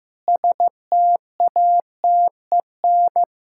Morse code, heard as a keyed tone, about 16 wpm, 700 Hz. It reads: STATEN
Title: Practice QSO - Exchange 1 - Beginning - 15wpm